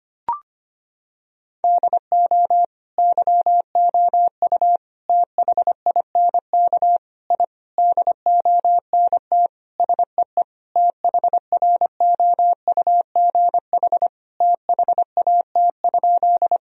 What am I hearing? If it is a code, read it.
Morse code, heard as a keyed tone, about 25 wpm, 700 Hz. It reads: DO YOU THINK I DONT SEE THROUGH THAT?
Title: Sentences from Top 1000 Words - Review Part 1 - 25wpm